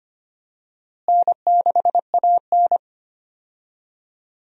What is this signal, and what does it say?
Morse code, heard as a keyed tone, about 25 wpm, 700 Hz. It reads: N6AN
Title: USA Call Signs - 4 Characters - Repeated 2x - 25wpm